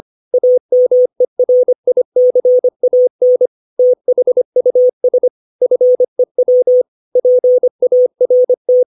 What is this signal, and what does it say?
Morse code, heard as a keyed tone, about 25 wpm, 500 Hz. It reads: AMERICAN THUS FEW PART